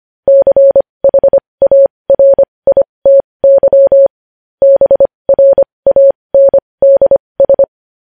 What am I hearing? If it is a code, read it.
Morse code, heard as a keyed tone, about 25 wpm, spaced as for 20 wpm, 550 Hz. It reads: CHARITY BRANDS
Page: News Headlines in Morse Code at 20 WPM